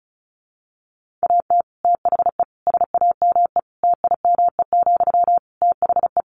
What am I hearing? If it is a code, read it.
Morse code, heard as a keyed tone, about 34 wpm, 700 Hz. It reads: AT THE SAME TIME, THE
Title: Sentences from Top 1500 Words 35wpm